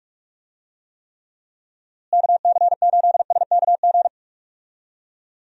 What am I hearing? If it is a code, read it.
Morse code, heard as a keyed tone, about 43 wpm, 700 Hz. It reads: KC8SKG
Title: USA Call Signs - 6 Characters - Repeated 2x - 45wpm